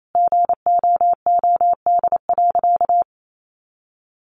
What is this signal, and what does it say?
Morse code, heard as a keyed tone, about 28 wpm, 700 Hz. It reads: GOOD.
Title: Sentences from Top 1200 Words 28wpm